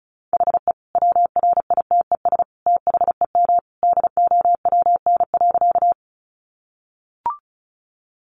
Morse code, heard as a keyed tone, about 34 wpm, 700 Hz. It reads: HE WRITES THEM DOWN.
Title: Additional Easy Phrases 35wpm